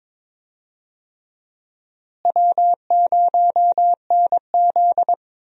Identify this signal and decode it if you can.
Morse code, heard as a keyed tone, about 22 wpm, 700 Hz. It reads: W0NZ
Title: USA Call Signs - 4 Characters - Repeated 3x - 22wpm